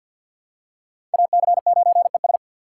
Morse code, heard as a keyed tone, about 51 wpm, 700 Hz. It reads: AC9ES
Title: USA Call Signs - 3x Character Spacing - 50wpm